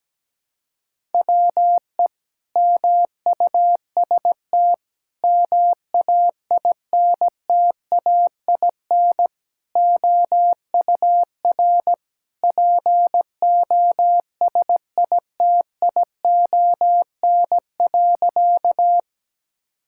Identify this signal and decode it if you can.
Morse code, heard as a keyed tone, about 17 wpm, 700 Hz. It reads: WE MUST MAINTAIN OUR POSITION.
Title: Sentences from Top 1000 Words - Review Part 5 - 17wpm